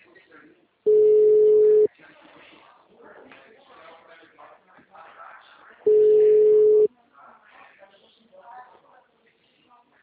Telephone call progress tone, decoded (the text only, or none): ringback tone